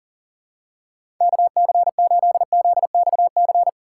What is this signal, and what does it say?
Morse code, heard as a keyed tone, about 40 wpm, 700 Hz. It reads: KC8ZXC
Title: USA Call Signs - 6 Characters - 2x Character Spacing - 40wpm